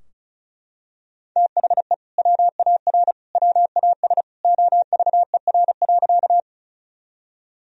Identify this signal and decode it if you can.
Morse code, heard as a keyed tone, about 36 wpm, 700 Hz. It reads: THE WAR WAS OVER.
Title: Sentences from Top 1000 Words - Review Part 2 - 35wpm